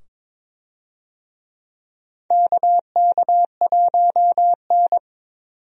Morse code, heard as a keyed tone, about 22 wpm, 700 Hz. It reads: KK1N
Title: USA Call Signs - 4 Characters - 22wpm